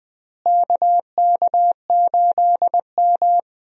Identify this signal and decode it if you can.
Morse code, heard as a keyed tone, about 20 wpm, 700 Hz. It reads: KK8M
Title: USA Call Signs - 4 Characters - Repeated 3x - 20wpm